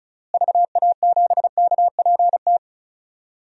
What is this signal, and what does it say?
Morse code, heard as a keyed tone, about 34 wpm, 700 Hz. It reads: VA7KPT
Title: DXCC-100 Call Signs - 6 Characters - Repeated 2x - 35wpm